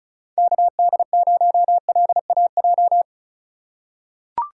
Morse code, heard as a keyed tone, about 36 wpm, 700 Hz. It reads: KD0LAJ
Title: USA Call Signs - 3x Character Spacing - 35wpm